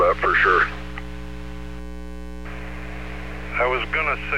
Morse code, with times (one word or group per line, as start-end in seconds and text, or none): none